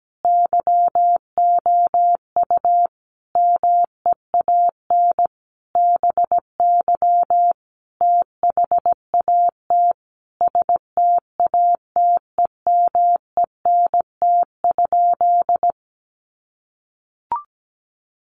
Morse code, heard as a keyed tone, about 17 wpm, 700 Hz. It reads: YOU MEAN BY THAT STATEMENT?
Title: Sentences from Top 1200 Words 17wpm